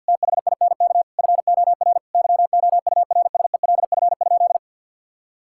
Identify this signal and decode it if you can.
Morse code, heard as a keyed tone, about 50 wpm, 700 Hz. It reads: THINK FOR YOURSELF?
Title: Sentences from Top 1000 Words - Review Part 6 - 50wpm